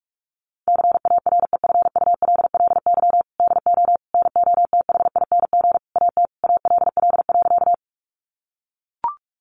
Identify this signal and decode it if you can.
Morse code, heard as a keyed tone, about 44 wpm, 700 Hz. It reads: CAREFULLY DO NOTHING AT ALL.
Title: Sentences from Top 1300 Words 45wpm